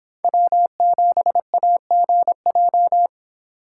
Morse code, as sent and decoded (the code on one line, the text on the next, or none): .-- --... .- --. .---
W7AGJ